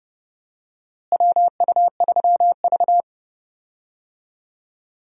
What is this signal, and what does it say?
Morse code, heard as a keyed tone, about 32 wpm, 700 Hz. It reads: WU3V
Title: USA Call Signs - 4 Characters - Repeated 3x - 30wpm